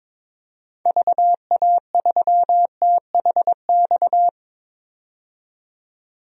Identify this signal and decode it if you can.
Morse code, heard as a keyed tone, about 22 wpm, 700 Hz. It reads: VA3THX
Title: DXCC-100 Call Signs - 6 Characters - 22wpm